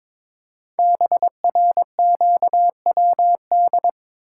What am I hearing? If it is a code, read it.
Morse code, heard as a keyed tone, about 23 wpm, 700 Hz. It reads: BRQWD
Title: Code Groups - 5 Letters - 22wpm